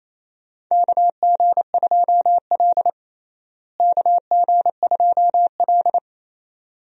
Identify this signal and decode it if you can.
Morse code, heard as a keyed tone, about 28 wpm, 700 Hz. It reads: KG2L KG2L
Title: USA Call Signs - 4 Characters - Repeated 2x - 28wpm